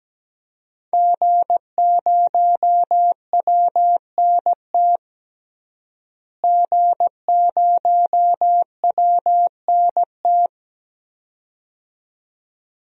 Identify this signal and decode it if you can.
Morse code, heard as a keyed tone, about 17 wpm, 700 Hz. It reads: G0WNT G0WNT
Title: DXCC-100 Call Signs - 5 Characters - Repeated 2x - 17wpm